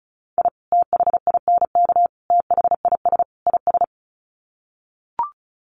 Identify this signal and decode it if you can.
Morse code, heard as a keyed tone, about 34 wpm, 700 Hz. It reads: I THINK THIS IS
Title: Sets of 4 Words 35wpm